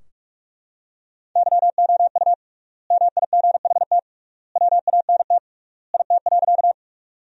Morse code, heard as a keyed tone, about 42 wpm, 700 Hz. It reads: YOU MIGHT WANT IT.
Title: 3x Word-Spacing - Sentences from Top 300 Words - 45wpm